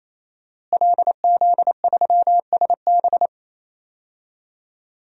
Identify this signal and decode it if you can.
Morse code, heard as a keyed tone, about 27 wpm, 700 Hz. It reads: LZ3SB
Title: DXCC-100 Call Signs - 5 Characters - Repeated 3x - 28wpm